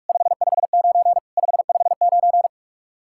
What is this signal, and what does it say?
Morse code, heard as a keyed tone, about 54 wpm, 700 Hz. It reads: 559 559
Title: Speed-Racing Common QSO Phrases - 45wpm